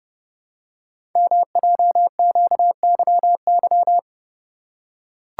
Morse code, heard as a keyed tone, about 30 wpm, 700 Hz. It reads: MJQYY
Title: Code Groups - 5 Letters - 30wpm